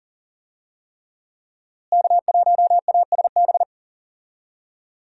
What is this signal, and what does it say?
Morse code, heard as a keyed tone, about 40 wpm, 700 Hz. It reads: K1ASB
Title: USA Call Signs - 5 Characters - Repeated 3x - 40wpm